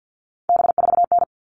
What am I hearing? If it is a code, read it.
Morse code, heard as a keyed tone, about 50 wpm, 700 Hz. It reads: B4N
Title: All Cut Numbers - 3-Digit 50wpm